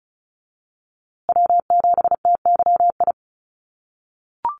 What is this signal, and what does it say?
Morse code, heard as a keyed tone, about 34 wpm, 700 Hz. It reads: W7TYI
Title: USA Call Signs Repeated 2x 35wpm